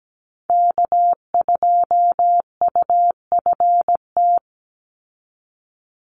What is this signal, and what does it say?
Morse code, heard as a keyed tone, about 17 wpm, 700 Hz. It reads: K2UFT
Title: Call Signs - 17wpm